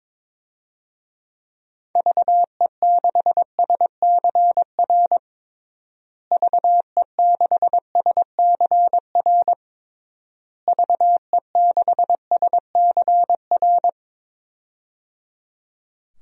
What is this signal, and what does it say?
Morse code, heard as a keyed tone, about 22 wpm, 700 Hz. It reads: VE6SCR VE6SCR VE6SCR
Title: DXCC-100 Call Signs - 6 Characters - Repeated 3x - 22wpm